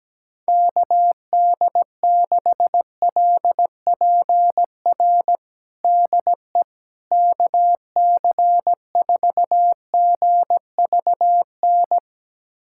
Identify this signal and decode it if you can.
Morse code, heard as a keyed tone, about 17 wpm, 700 Hz. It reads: KD6LPR DE KC4GVN
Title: USA Callsign DE Callsign - 6 Characters - 17wpm